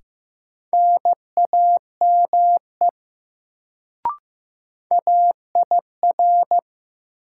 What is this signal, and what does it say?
Morse code, heard as a keyed tone, about 15 wpm, 700 Hz. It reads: NAME AIR
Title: Sentences from Top 200 Words - New Word Drill - 15wpm